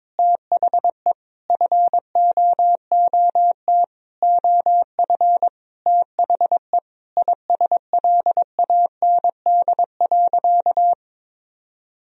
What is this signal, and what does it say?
Morse code, heard as a keyed tone, about 22 wpm, 700 Hz. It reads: THE FOOT OF THE ISLAND.
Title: Sentences from Top 1000 Words - Review Part 3 - 22wpm